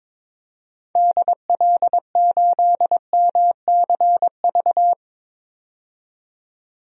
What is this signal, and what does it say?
Morse code, heard as a keyed tone, about 22 wpm, 700 Hz. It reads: DL8MCV
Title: DXCC-100 Call Signs - 6 Characters - 22wpm